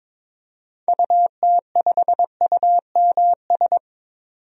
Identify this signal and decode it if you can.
Morse code, heard as a keyed tone, about 22 wpm, 700 Hz. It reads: UT5UMS
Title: DXCC-100 Call Signs - 6 Characters - 22wpm